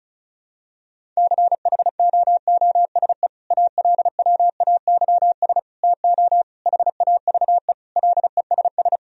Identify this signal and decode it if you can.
Morse code, heard as a keyed tone, about 34 wpm, 700 Hz. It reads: CHOOSE ALWAYS TO HAVE LESS
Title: Sentences from Top 1000 Words - Review Part 4 - 35wpm